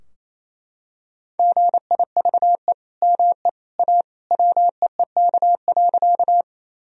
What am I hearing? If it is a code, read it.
Morse code, heard as a keyed tone, about 28 wpm, 700 Hz. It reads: GIVE ME A WEEK.
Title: Sentences from Top 1000 Words - Review Part 2 - 28wpm